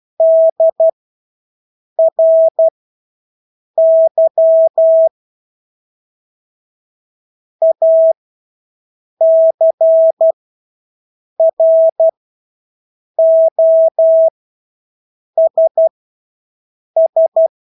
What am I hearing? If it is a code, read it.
Morse code, heard as a keyed tone, about 12 wpm, spaced as for 6 wpm, 650 Hz. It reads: DRY ACROSS
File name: Engelske ord 12-6 wpm